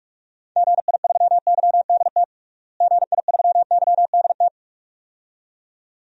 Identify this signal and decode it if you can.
Morse code, heard as a keyed tone, about 44 wpm, 700 Hz. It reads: GI3YDT GI3YDT
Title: DXCC-100 Call Signs - 6 Characters - Repeated 2x - 45wpm